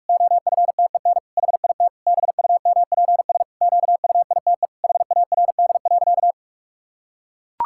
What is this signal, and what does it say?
Morse code, heard as a keyed tone, about 41 wpm, 700 Hz. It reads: OFTEN HIT BUMPS QUITE HARD.
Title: Pangram Challenge v2 - 45wpm